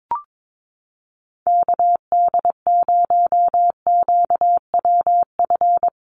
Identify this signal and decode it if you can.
Morse code, heard as a keyed tone, about 22 wpm, 700 Hz. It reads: KD0QWF
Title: USA Call Signs Repeated 3x 22wpm